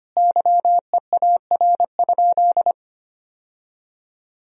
Morse code, heard as a keyed tone, about 25 wpm, 700 Hz. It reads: YEAR?
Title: Sentences from Top 1000 Words - Review Part 2 - 25wpm